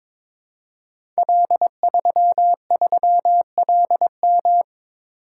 Morse code, heard as a keyed tone, about 22 wpm, 700 Hz. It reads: L33LM
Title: Code Groups - 5 Letter:Numbers - 22wpm